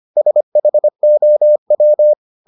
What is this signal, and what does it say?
Morse code, heard as a keyed tone, about 25 wpm, 600 Hz. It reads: SHOW